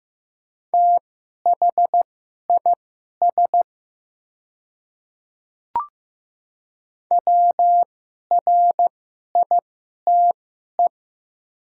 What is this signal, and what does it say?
Morse code, heard as a keyed tone, about 15 wpm, spaced as for 11 wpm, 700 Hz. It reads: THIS WRITE
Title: 2x Character Spacing - Top 100 Words - 15wpm